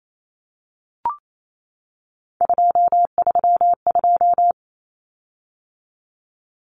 Morse code, heard as a keyed tone, about 28 wpm, 700 Hz. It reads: 232
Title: Common Cut Numbers - 3-Digit 28wpm